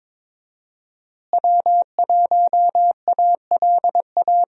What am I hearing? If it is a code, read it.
Morse code, heard as a keyed tone, about 22 wpm, 700 Hz. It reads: W1ALA
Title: USA Call Signs - 5 Characters - Repeated 3x - 22wpm